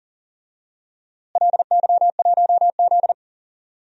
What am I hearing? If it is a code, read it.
Morse code, heard as a keyed tone, about 37 wpm, 700 Hz. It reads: LY1Z
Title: DXCC-100 Call Signs - 4 Characters - Repeated 3x - 40wpm